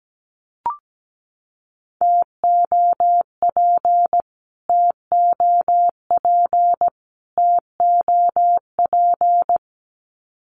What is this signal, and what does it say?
Morse code, heard as a keyed tone, about 17 wpm, 700 Hz. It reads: TOP TOP TOP
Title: Top 500 Words Repeated 17wpm